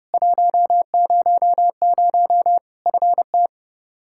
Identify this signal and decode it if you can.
Morse code, heard as a keyed tone, about 32 wpm, 700 Hz. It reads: 100 FT
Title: Practice QSO - Exchange 2 - Info - 30wpm